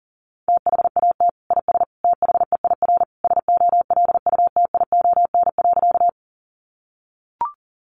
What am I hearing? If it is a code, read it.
Morse code, heard as a keyed tone, about 40 wpm, 700 Hz. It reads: THAT IS THEIR SOLUTION.
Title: Sentences from Top 1300 Words 40wpm